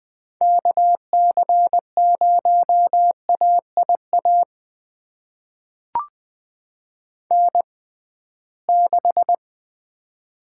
Morse code, heard as a keyed tone, about 20 wpm, spaced as for 15 wpm, 700 Hz. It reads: KC0AIA N6
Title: USA Call Signs - 6x Character Spacing - 20wpm